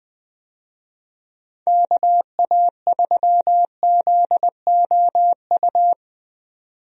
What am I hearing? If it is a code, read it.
Morse code, heard as a keyed tone, about 20 wpm, 700 Hz. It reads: KA3ZOU
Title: USA Call Signs - 6 Characters - 20wpm